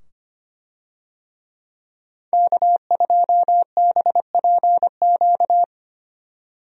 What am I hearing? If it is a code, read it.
Morse code, heard as a keyed tone, about 25 wpm, 700 Hz. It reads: K2BPQ
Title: USA Call Signs - 5 Characters - 25wpm